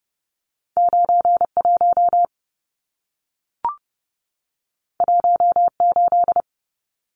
Morse code, heard as a keyed tone, about 30 wpm, 700 Hz. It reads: 91 18
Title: Numbers - 2-Digit 30wpm